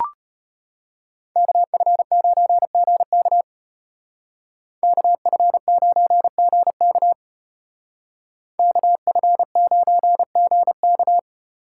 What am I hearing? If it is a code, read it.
Morse code, heard as a keyed tone, about 33 wpm, 700 Hz. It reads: KF9GK KF9GK KF9GK